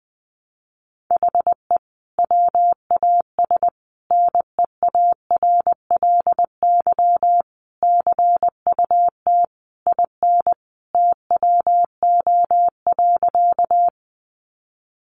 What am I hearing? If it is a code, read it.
Morse code, heard as a keyed tone, about 20 wpm, 700 Hz. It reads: HE WAS NEARLY CUT IN TWO.
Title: Sentences from Top 1300 Words 20wpm